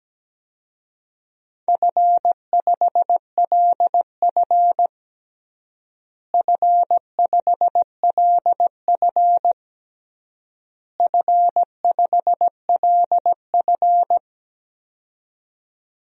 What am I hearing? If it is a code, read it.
Morse code, heard as a keyed tone, about 17 wpm, 700 Hz. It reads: F5LF F5LF F5LF
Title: DXCC-100 Call Signs - 4 Characters - Repeated 3x - 17wpm